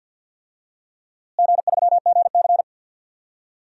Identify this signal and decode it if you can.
Morse code, heard as a keyed tone, about 52 wpm, 700 Hz. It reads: G3GC
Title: DXCC-100 Call Signs - 4 Characters - Repeated 2x - 50wpm